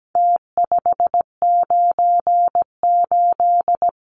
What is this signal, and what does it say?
Morse code, heard as a keyed tone, about 17 wpm, 700 Hz. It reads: T598
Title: Numbers - Contest Style - 4-Digit 17wpm